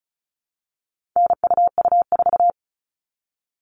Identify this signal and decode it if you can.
Morse code, heard as a keyed tone, about 34 wpm, 700 Hz. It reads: NUU4
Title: All Cut Numbers - 4-Digit 35wpm